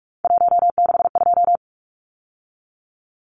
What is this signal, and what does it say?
Morse code, heard as a keyed tone, about 44 wpm, 700 Hz. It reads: 162
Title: Numbers - 3-Digit 45wpm